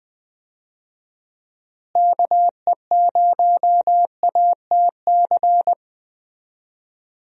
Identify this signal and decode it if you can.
Morse code, heard as a keyed tone, about 20 wpm, 700 Hz. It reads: KE0ATC
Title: USA Call Signs - 6 Characters - Repeated 3x - 20wpm